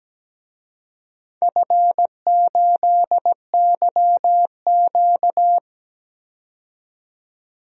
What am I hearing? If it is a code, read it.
Morse code, heard as a keyed tone, about 17 wpm, 700 Hz. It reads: F8YQ
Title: DXCC-100 Call Signs - 4 Characters - 17wpm